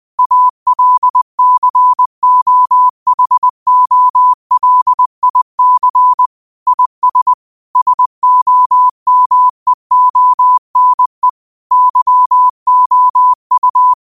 Morse code, heard as a keyed tone, about 20 wpm, 1000 Hz. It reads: ALCOHOLIC IS SOMEONE YOU
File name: qotdfiles / 20wpm